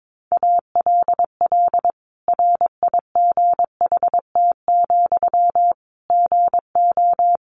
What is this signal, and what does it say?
Morse code, heard as a keyed tone, about 22 wpm, 700 Hz. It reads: ALL RIGHT, GO
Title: Sentences from Top 1100 Words 22wpm